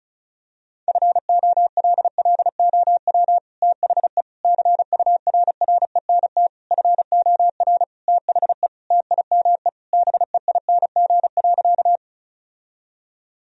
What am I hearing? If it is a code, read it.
Morse code, heard as a keyed tone, about 34 wpm, 700 Hz. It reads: FOLLOW THE CURRENT FOR THE TIME BEING.
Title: Sentences from Top 1000 Words - Review Part 4 - 35wpm